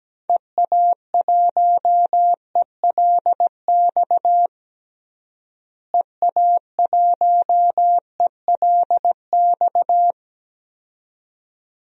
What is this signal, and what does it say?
Morse code, heard as a keyed tone, about 17 wpm, 700 Hz. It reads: EA1ELX EA1ELX
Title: DXCC-100 Call Signs - 6 Characters - Repeated 2x - 17wpm